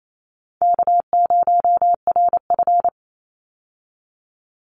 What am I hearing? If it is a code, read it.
Morse code, heard as a keyed tone, about 28 wpm, 700 Hz. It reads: K0RF
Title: Easy Call Signs Repeated 3x - 28wpm